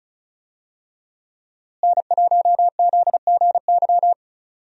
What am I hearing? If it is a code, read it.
Morse code, heard as a keyed tone, about 34 wpm, 700 Hz. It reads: N1ZGY
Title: USA Call Signs - 5 Characters - Repeated 2x - 35wpm